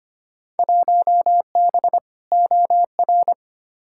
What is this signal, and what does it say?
Morse code, heard as a keyed tone, about 25 wpm, 700 Hz. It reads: 1B OR
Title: Field Day Exchanges Encore - 25wpm